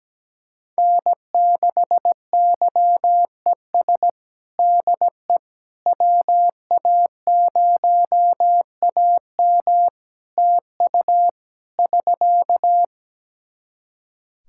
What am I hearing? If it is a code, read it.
Morse code, heard as a keyed tone, about 17 wpm, 700 Hz. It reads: N6YES DE WA0AM TU <SK>
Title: Practice QSO - Exchange 4 - Ending - 17wpm